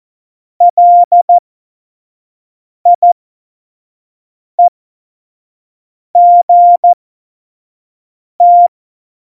Morse code, heard as a keyed tone, about 14 wpm, spaced as for 5 wpm, 700 Hz. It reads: LIEGT
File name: Morsekurs / Witze / Witz 13 March 2018, 25 F70